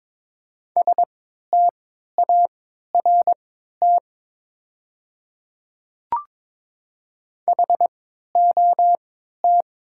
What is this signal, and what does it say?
Morse code, heard as a keyed tone, about 22 wpm, spaced as for 12 wpm, 700 Hz. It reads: START HOT
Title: 3x Character Spacing - Top 500 Words - 22wpm